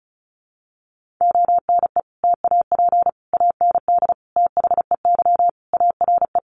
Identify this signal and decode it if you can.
Morse code, heard as a keyed tone, about 34 wpm, 700 Hz. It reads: ONE TAP AND THEY ARE
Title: Sentences from Top 1400 Words 35wpm